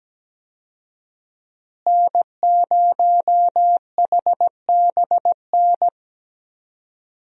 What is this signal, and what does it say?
Morse code, heard as a keyed tone, about 17 wpm, 700 Hz. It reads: N0HBN
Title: USA Call Signs - 5 Characters - Repeated 3x - 17wpm